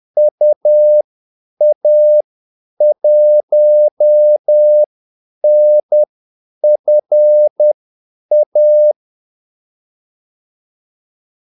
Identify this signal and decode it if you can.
Morse code, heard as a keyed tone, about 10 wpm, spaced as for 8 wpm, 600 Hz. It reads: UA1NFA
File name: Callsigns-10-8wpm